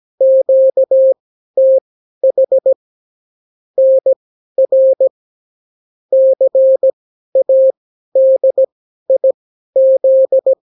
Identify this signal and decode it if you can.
Morse code, heard as a keyed tone, about 17 wpm, spaced as for 12 wpm, 530 Hz. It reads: QTH NR CADIZ
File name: CW QSO libroCW4 18-8ppm